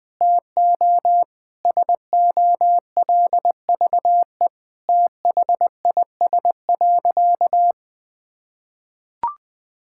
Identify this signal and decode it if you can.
Morse code, heard as a keyed tone, about 20 wpm, 700 Hz. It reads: TO SOLVE THIS.
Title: Sentences with 6-Word Phrases - 20wpm